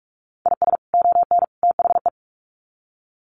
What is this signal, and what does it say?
Morse code, heard as a keyed tone, about 44 wpm, 700 Hz. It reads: IS ON THE
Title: Sets of 3 Words 45wpm